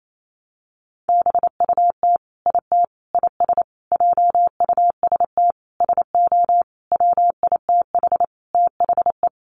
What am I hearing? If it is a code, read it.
Morse code, heard as a keyed tone, about 28 wpm, 700 Hz. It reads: BUT IT IS JUST SO WITH THE
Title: Sentences from Top 200 Words 28wpm